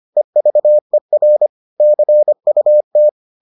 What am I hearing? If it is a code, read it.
Morse code, heard as a keyed tone, about 25 wpm, 600 Hz. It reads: EVER CUT